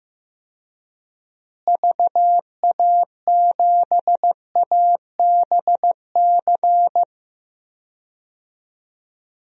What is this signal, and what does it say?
Morse code, heard as a keyed tone, about 15 wpm, 700 Hz. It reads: VA7ABC
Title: DXCC-100 Call Signs - 6 Characters - Repeated 2x - 15wpm